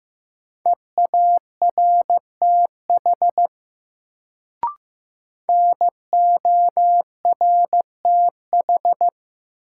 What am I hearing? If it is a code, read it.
Morse code, heard as a keyed tone, about 15 wpm, 700 Hz. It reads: EARTH NORTH
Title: Sets of 5 Words Encore - New Word Drill - 15wpm